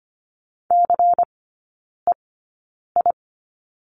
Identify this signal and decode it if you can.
Morse code, heard as a keyed tone, about 25 wpm, spaced as for 9 wpm, 700 Hz. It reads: CEI
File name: course / 25:10wpm / 093 Lesson 16e - Letter c - Three Characters 25wpm